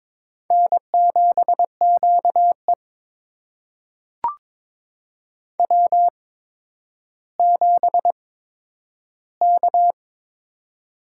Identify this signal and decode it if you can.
Morse code, heard as a keyed tone, about 22 wpm, spaced as for 11 wpm, 700 Hz. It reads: N7QE W7K
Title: USA Call Signs - 4 Characters - 8x Character Spacing - 22wpm